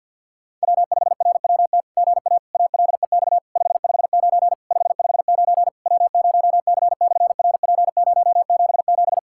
Morse code, heard as a keyed tone, about 53 wpm, 700 Hz. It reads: W5RWT GA ALEX 559 559 W0C/RP077
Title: SOTA QSO Practice 50wpm